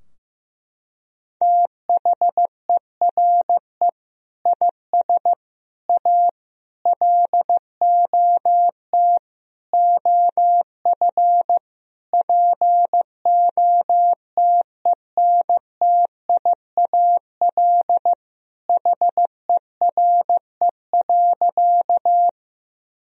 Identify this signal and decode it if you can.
Morse code, heard as a keyed tone, about 15 wpm, 700 Hz. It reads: THERE IS A LOT OF POTENTIAL HERE.
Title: Sentences from Top 1000 Words - Review Part 4 - 15wpm